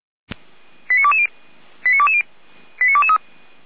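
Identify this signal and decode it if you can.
Five-tone selective call (ZVEI), heard as quick runs of five tones, tones of about 70 ms. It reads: 89220 89220 89223
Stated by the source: ZVEI five-tone call received by SDR.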